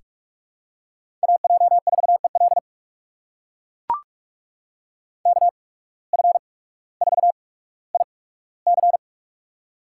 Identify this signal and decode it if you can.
Morse code, heard as a keyed tone, about 41 wpm, spaced as for 19 wpm, 700 Hz. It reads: AJ4EL KF4IC
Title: USA Call Signs - 5 Characters - 8x Character Spacing - 45wpm